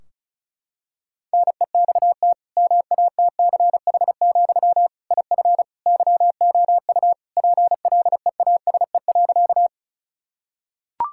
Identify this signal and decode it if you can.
Morse code, heard as a keyed tone, about 36 wpm, 700 Hz. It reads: NEXT MATCH, IF YOU PLEASE.
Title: Sentences from Top 1600 Words 35wpm